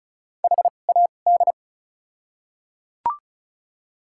Morse code, heard as a keyed tone, about 36 wpm, spaced as for 25 wpm, 700 Hz. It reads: HAD
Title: 2x Character Spacing - Top 100 Words - 35wpm